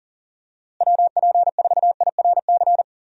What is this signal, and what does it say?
Morse code, heard as a keyed tone, about 37 wpm, 700 Hz. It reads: WP4IRC
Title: USA Call Signs - 6 Characters - Repeated 3x - 40wpm